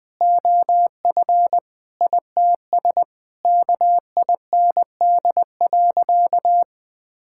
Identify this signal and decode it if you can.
Morse code, heard as a keyed tone, about 20 wpm, 700 Hz. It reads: OF ITS KIND.
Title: Sentences with 6-Word Phrases - 20wpm